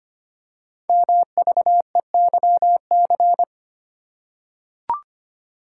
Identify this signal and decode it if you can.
Morse code, heard as a keyed tone, about 25 wpm, 700 Hz. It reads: MVEYC
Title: Code Groups - 5 Letters - 25wpm